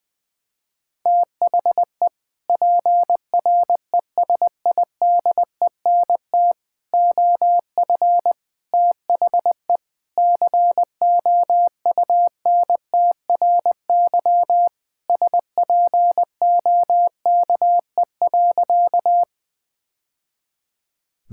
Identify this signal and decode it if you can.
Morse code, heard as a keyed tone, about 20 wpm, 700 Hz. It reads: THE PRESIDENT OF THE COUNTRY SPOKE.
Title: Sentences with 3-Word Phrases - 20wpm